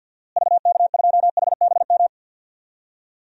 Morse code, heard as a keyed tone, about 52 wpm, 700 Hz. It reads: VK2HBM
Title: DXCC-100 Call Signs - 6 Characters - Repeated 3x - 50wpm